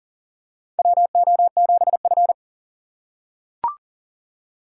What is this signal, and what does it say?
Morse code, heard as a keyed tone, about 40 wpm, 700 Hz. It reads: WO7F
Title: USA Call Signs - 4 Characters - 6x Character Spacing - 40wpm